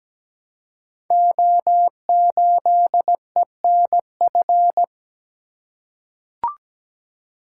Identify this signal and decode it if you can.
Morse code, heard as a keyed tone, about 17 wpm, 700 Hz. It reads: O8ENF
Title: Code Groups - 5 Letter:Numbers - 17wpm